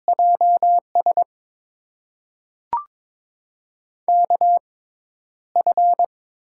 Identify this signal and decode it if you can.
Morse code, heard as a keyed tone, about 22 wpm, spaced as for 11 wpm, 700 Hz. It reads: JS KF
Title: USA Call Signs - 6x Character Spacing - 22wpm